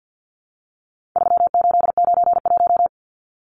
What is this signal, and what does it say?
Morse code, heard as a keyed tone, about 50 wpm, 700 Hz. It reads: 3891
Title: Numbers - 4-Digit 50wpm